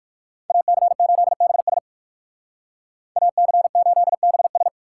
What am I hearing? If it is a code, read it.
Morse code, heard as a keyed tone, about 61 wpm, 700 Hz. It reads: AC8BS AC8BS